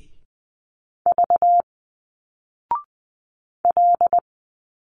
Morse code, heard as a keyed tone, about 20 wpm, 700 Hz. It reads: V L
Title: Single Letters 20wpm